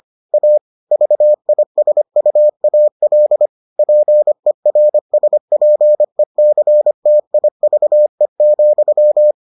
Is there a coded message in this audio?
Yes: Morse code from 0.3 to 9.3 s, heard as a keyed tone, about 25 wpm, 600 Hz. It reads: A VISUAL PERSPECTIVE,